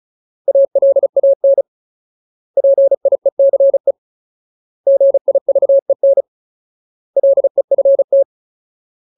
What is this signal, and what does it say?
Morse code, heard as a keyed tone, about 35 wpm, 550 Hz. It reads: ALAN PIECE GIVEN LEFT
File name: Ultimate Word List / 35WPM Ultimate